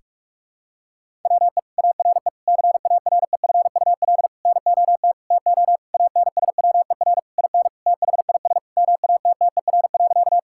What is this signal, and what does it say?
Morse code, heard as a keyed tone, about 41 wpm, 700 Hz. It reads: WE ARE CAREFUL NOT TO ANSWER IN THIS MATTER.
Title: Sentences from Top 1600 Words 45wpm